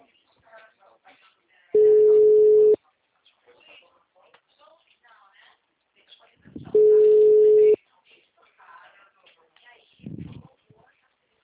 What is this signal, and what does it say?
Ringback tone (Europe, 425 Hz), with speech.
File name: out- -2008- - - .42452